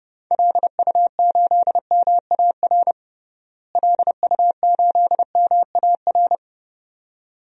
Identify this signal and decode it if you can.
Morse code, heard as a keyed tone, about 32 wpm, 700 Hz. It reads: LU8MAR LU8MAR
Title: DXCC-100 Call Signs - 6 Characters - Repeated 2x - 30wpm